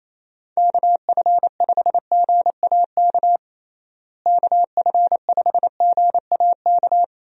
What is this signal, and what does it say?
Morse code, heard as a keyed tone, about 28 wpm, 700 Hz. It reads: KF5GAK KF5GAK
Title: USA Call Signs - 6 Characters - Repeated 3x - 28wpm